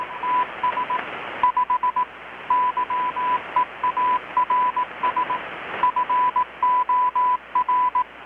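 Morse code, heard as a keyed tone, about 18 wpm, 1000 Hz, with radio noise. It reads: TS 5 YEARS FOR
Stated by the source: Morse from KPH's Night of Nights XX maritime radiotelegraphy event, received in Chiapas, Mexico about 3,800 km away on a Kenwood R-600 with a grounded 10 m long-wire antenna.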